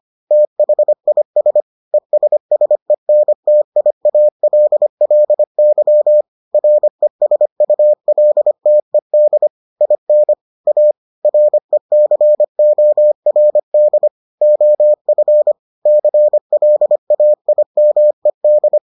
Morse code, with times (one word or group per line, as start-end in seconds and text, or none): 0.3-1.6 THIS
1.9-6.2 ESSENTIALLY
6.5-9.5 RESULTED
9.8-10.3 IN
10.7-10.9 A
11.2-14.1 RECORD
14.4-15.5 OF
15.9-18.8 CLAIMED